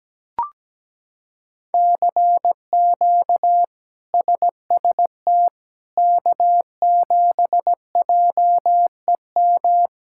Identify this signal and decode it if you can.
Morse code, heard as a keyed tone, about 17 wpm, 700 Hz. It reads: CQ SST K7JEM
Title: K1USN SST QSO Practice - 17wpm